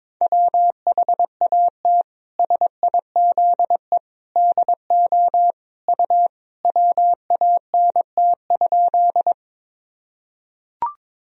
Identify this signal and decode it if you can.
Morse code, heard as a keyed tone, about 22 wpm, 700 Hz. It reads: WHAT SIZE DO U WANT?
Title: Sentences from Top 400 Words and Abbreviations 22wpm